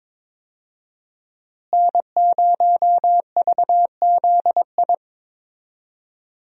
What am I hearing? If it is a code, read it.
Morse code, heard as a keyed tone, about 22 wpm, 700 Hz. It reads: N0VZI
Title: USA Call Signs - 5 Characters - Repeated 2x - 22wpm